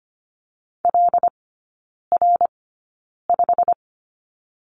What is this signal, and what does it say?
Morse code, heard as a keyed tone, about 25 wpm, spaced as for 9 wpm, 690 Hz. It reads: LR5
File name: Lesson 15e - Number 5 - Three Characters 25wpm